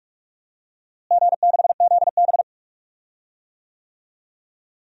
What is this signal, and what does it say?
Morse code, heard as a keyed tone, about 46 wpm, 700 Hz. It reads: G6ZB
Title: DXCC-100 Call Signs - 4 Characters - Repeated 3x - 45wpm